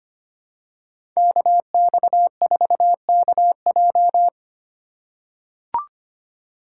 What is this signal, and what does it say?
Morse code, heard as a keyed tone, about 25 wpm, 700 Hz. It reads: KX4KJ
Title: USA Call Signs - 5 Characters - 4x Character Spacing - 25wpm